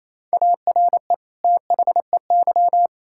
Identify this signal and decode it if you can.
Morse code, heard as a keyed tone, about 28 wpm, 700 Hz. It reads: ARE THEY
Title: Sentences from Top 1000 Words - Review Part 5 - 28wpm